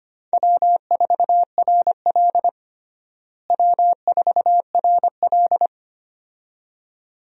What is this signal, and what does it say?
Morse code, heard as a keyed tone, about 25 wpm, 700 Hz. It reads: W4RL W4RL
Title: USA Call Signs - 4 Characters - Repeated 2x - 25wpm